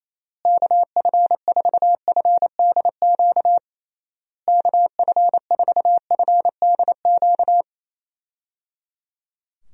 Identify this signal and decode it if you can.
Morse code, heard as a keyed tone, about 28 wpm, 700 Hz. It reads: KF4FDQ KF4FDQ
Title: USA Call Signs - 6 Characters - Repeated 3x - 28wpm